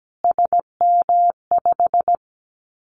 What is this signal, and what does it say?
Morse code, heard as a keyed tone, about 17 wpm, 700 Hz. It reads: SM5
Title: Callsign Prefixes for Top 25 DXCC - Repeated 3x - 17wpm